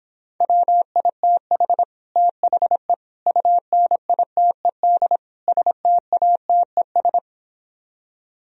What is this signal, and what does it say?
Morse code, heard as a keyed tone, about 29 wpm, 700 Hz. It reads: WITH THE UNITED STATES